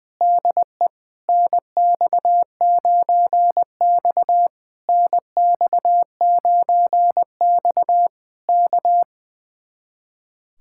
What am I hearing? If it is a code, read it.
Morse code, heard as a keyed tone, about 20 wpm, 700 Hz. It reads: DE NX9X NX9X K
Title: Speed-Racing Common QSO Phrases Encore - 20wpm